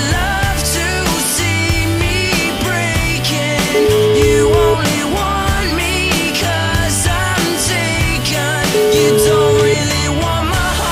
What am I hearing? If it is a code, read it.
Ringback tone (Europe, 425 Hz), with noise.